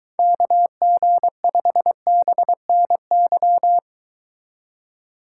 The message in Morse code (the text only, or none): KG5BNY